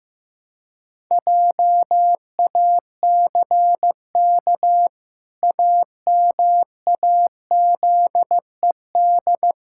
Morse code, heard as a keyed tone, about 15 wpm, 700 Hz. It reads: JACK AMAZED
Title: Pangram Challenge v2 - 15wpm